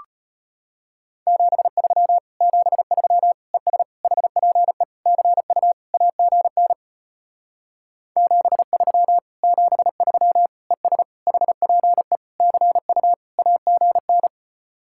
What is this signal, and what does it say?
Morse code, heard as a keyed tone, about 33 wpm, 700 Hz. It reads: 73 73 ES HPE CU AGN 73 73 ES HPE CU AGN